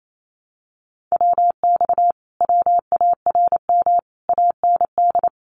Morse code, heard as a keyed tone, about 28 wpm, 700 Hz. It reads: WX WARM AND
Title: Common QSO Phrases Encore 28wpm